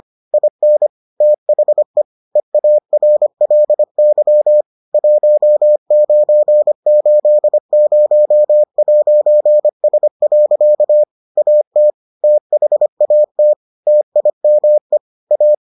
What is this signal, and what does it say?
Morse code, heard as a keyed tone, about 25 wpm, 600 Hz. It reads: IN THE EARLY 1980'S. AT THAT TIME A